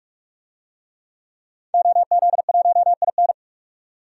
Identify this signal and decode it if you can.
Morse code, heard as a keyed tone, about 41 wpm, 700 Hz. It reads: OZ1IN
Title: DXCC-100 Call Signs - 5 Characters - 45wpm